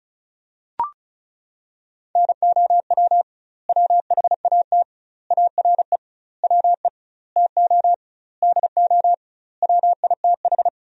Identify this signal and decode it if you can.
Morse code, heard as a keyed tone, about 36 wpm, 700 Hz. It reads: NOW WHAT ARE WE TO DO WITH
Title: 2x Word-Spacing - Sentences from Top 100 Words - 35wpm